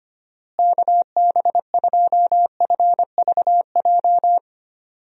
Morse code, heard as a keyed tone, about 25 wpm, 700 Hz. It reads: KB2FVJ
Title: USA Call Signs - 6 Characters - Repeated 2x - 25wpm